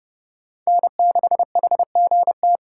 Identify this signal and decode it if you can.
Morse code, heard as a keyed tone, about 30 wpm, 700 Hz. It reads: N6HGT
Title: USA Call Signs - 5 Characters - 4x Character Spacing - 30wpm